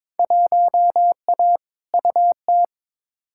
Morse code, heard as a keyed tone, about 22 wpm, 700 Hz. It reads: 1A UT
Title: Field Day Exchanges - 22wpm